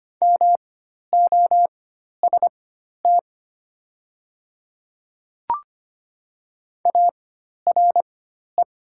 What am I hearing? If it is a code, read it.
Morse code, heard as a keyed tone, about 25 wpm, spaced as for 12 wpm, 700 Hz. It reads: MOST ARE
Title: 4x Character Spacing - Top 100 Words - 25wpm